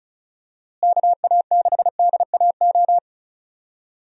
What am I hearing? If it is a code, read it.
Morse code, heard as a keyed tone, about 36 wpm, 700 Hz. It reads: KA6DAO
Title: USA Call Signs - 6 Characters - Repeated 2x - 35wpm